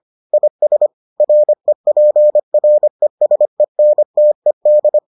Morse code, heard as a keyed tone, about 25 wpm, 600 Hz. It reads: IS REPRESENTED